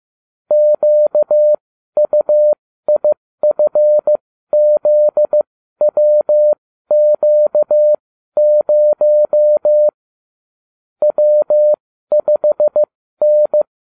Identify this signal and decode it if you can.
Morse code, heard as a keyed tone, about 15 wpm, spaced as for 12 wpm, 600 Hz. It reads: QUIFZWQ0 W5N
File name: cw / ubacw29 2